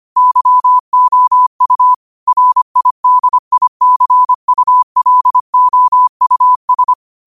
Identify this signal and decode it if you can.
Morse code, heard as a keyed tone, about 25 wpm, 1000 Hz. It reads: YOU RIDICULOUS
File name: 25wpm